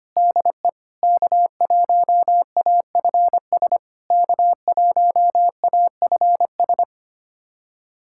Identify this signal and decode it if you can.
Morse code, heard as a keyed tone, about 25 wpm, 690 Hz. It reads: DE K1AFS K1AFS
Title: SOTA QSO Practice 25wpm